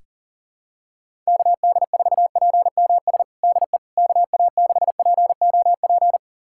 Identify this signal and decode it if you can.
Morse code, heard as a keyed tone, about 40 wpm, 700 Hz. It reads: KD4PMS DE KA6POP
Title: USA Callsign DE Callsign 40wpm